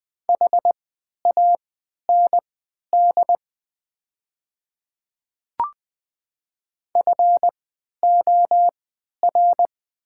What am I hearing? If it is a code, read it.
Morse code, heard as a keyed tone, about 20 wpm, spaced as for 11 wpm, 700 Hz. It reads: HAND FOR
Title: 3x Character Spacing - Top 200 Words - 20wpm